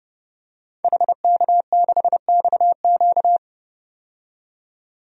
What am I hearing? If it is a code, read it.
Morse code, heard as a keyed tone, about 30 wpm, 700 Hz. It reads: HK6XQ
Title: DXCC-100 Call Signs - 5 Characters - 30wpm